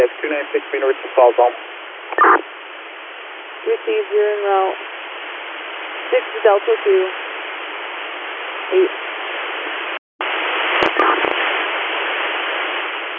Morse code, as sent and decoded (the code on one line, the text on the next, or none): none